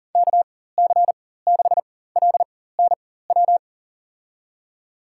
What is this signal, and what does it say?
Morse code, heard as a keyed tone, about 40 wpm, spaced as for 19 wpm, 700 Hz. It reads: KC6LNW
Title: USA Call Signs - 6 Characters - 4x Character Spacing - 40wpm